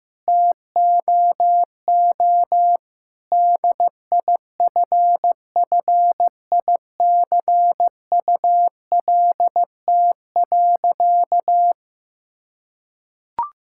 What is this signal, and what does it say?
Morse code, heard as a keyed tone, about 15 wpm, 700 Hz. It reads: TOO DIFFICULT.
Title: Sentences from Top 1000 Words - Review Part 5 - 15wpm